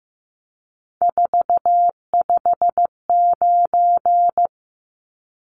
Morse code, heard as a keyed tone, about 15 wpm, 700 Hz. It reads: 459
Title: Numbers - 3-Digit 15wpm